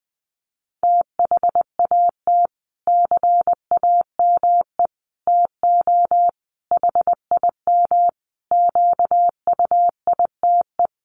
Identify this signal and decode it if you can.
Morse code, heard as a keyed tone, about 20 wpm, 700 Hz. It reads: THAT CAME TO HIM QUITE
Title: Sentences from Top 1300 Words 20wpm